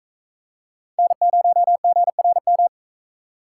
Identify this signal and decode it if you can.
Morse code, heard as a keyed tone, about 41 wpm, 700 Hz. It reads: N0GRM